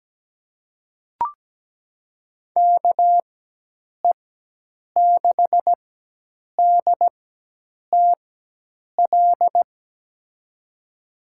Morse code, heard as a keyed tone, about 17 wpm, spaced as for 8 wpm, 700 Hz. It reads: KE6DTL
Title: USA Call Signs - 4x Character Spacing - 17wpm